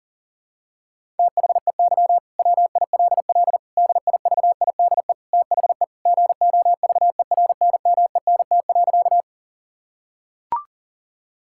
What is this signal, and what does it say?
Morse code, heard as a keyed tone, about 37 wpm, 700 Hz. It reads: THEY WILL DIVIDE THE GOVERNMENT.
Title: Sentences from Top 1600 Words 40wpm